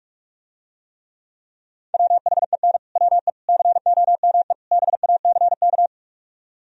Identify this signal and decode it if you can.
Morse code, heard as a keyed tone, about 47 wpm, 700 Hz. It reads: WHEN WE COME BACK